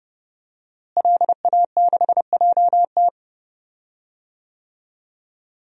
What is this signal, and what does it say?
Morse code, heard as a keyed tone, about 30 wpm, 700 Hz. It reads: LA6JT
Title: DXCC-100 Call Signs - 5 Characters - Repeated 3x - 30wpm